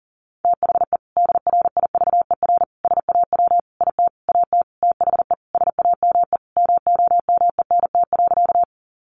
Morse code, heard as a keyed tone, about 40 wpm, 700 Hz. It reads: THE DRIVER SAW IT AT THE SAME MOMENT.
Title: Sentences from Top 1000 Words 40wpm